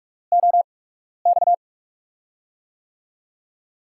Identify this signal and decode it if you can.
Morse code, heard as a keyed tone, about 44 wpm, spaced as for 12 wpm, 700 Hz. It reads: OX
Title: USA Call Signs - 6 Characters - 8x Character Spacing - 45wpm